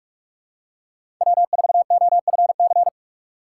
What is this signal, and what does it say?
Morse code, heard as a keyed tone, about 46 wpm, 700 Hz. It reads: W4OFC
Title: USA Call Signs - 5 Characters - 2x Character Spacing - 45wpm